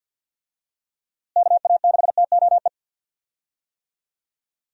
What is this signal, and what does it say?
Morse code, heard as a keyed tone, about 51 wpm, 700 Hz. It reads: KA6TOE
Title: USA Call Signs - 6 Characters - Repeated 2x - 50wpm